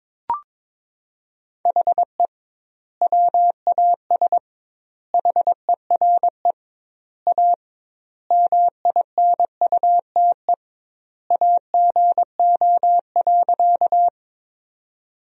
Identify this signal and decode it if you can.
Morse code, heard as a keyed tone, about 22 wpm, 700 Hz. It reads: HE WAS HERE A MINUTE AGO.
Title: 2x Word-Spacing - Sentences from Top 500 Words - 22wpm